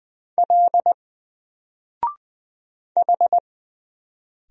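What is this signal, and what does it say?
Morse code, heard as a keyed tone, about 20 wpm, 700 Hz. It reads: L H
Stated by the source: Single letter and number Morse drill at 20 wpm, from N4LCD.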